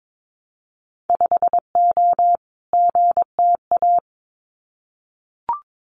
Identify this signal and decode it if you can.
Morse code, heard as a keyed tone, about 22 wpm, 700 Hz. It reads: 5O GTA
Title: Winter Field Day Exchanges Encore 22wpm